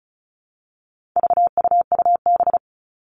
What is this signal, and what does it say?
Morse code, heard as a keyed tone, about 34 wpm, 700 Hz. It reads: VUUB
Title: All Cut Numbers - 4-Digit 35wpm